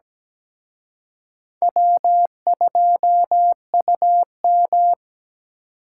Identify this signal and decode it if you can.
Morse code, heard as a keyed tone, about 20 wpm, 700 Hz. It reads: W2UM
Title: Speed Racing - USA 1x2 Call Signs 17wpm